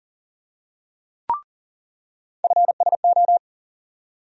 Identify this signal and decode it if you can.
Morse code, heard as a keyed tone, about 40 wpm, 700 Hz. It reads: FSO
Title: USA Call Sign Postfixes - 3 Letters 40wpm